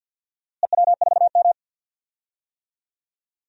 Morse code, heard as a keyed tone, about 47 wpm, 700 Hz. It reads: EW4M
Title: DXCC-100 Call Signs - 4 Characters - Repeated 3x - 50wpm